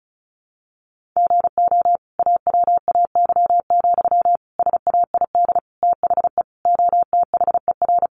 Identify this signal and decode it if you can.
Morse code, heard as a keyed tone, about 34 wpm, 700 Hz. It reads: GO AWAY, SAID THE OTHER
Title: Sentences from Top 1500 Words 35wpm